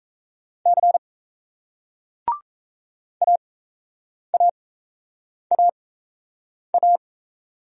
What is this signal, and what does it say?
Morse code, heard as a keyed tone, about 38 wpm, spaced as for 8 wpm, 700 Hz. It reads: C AAAA